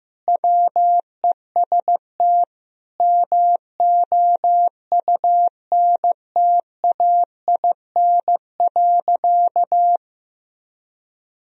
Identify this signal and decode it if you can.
Morse code, heard as a keyed tone, about 15 wpm, 700 Hz. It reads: WEST MOUNTAIN.
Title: Sentences from Top 1000 Words - Review Part 2 - 15wpm